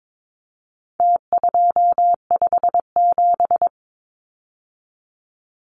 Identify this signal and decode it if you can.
Morse code, heard as a keyed tone, about 22 wpm, 700 Hz. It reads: T257
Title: Numbers - Contest Style - 4-Digit 22wpm